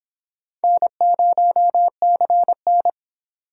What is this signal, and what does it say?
Morse code, heard as a keyed tone, about 26 wpm, 700 Hz. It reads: N0CN